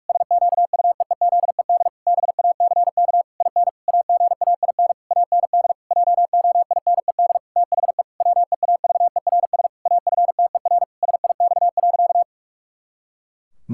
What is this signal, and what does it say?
Morse code, heard as a keyed tone, about 46 wpm, 700 Hz. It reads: SQUEEZED BACK IN AGAIN AND JOINED THE WEAVERS AFTER SIX.
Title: Pangram Challenge v2 - 45wpm